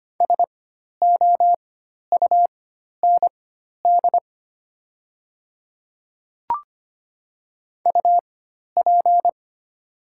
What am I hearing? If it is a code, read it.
Morse code, heard as a keyed tone, about 25 wpm, spaced as for 12 wpm, 700 Hz. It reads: SOUND UP
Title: 4x Character Spacing - Top 100 Words - 25wpm